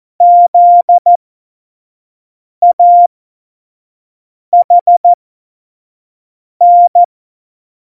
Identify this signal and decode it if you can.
Morse code, heard as a keyed tone, about 14 wpm, spaced as for 5 wpm, 700 Hz. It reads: ZAHN